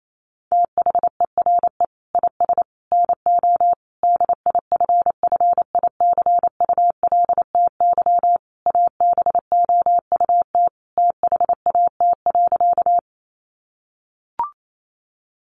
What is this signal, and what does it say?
Morse code, heard as a keyed tone, about 28 wpm, 700 Hz. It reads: THERE IS NO DIFFICULTY ABOUT THAT.
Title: Sentences from Top 1500 Words 28wpm